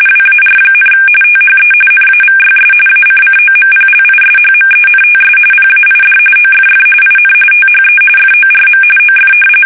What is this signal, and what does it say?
RTTY, about 45.45 baud. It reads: WAR, THE NAVYS ACCOMPLISHMENTS PAID OFF IN THE FORM OF
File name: fsk850 / usnavy2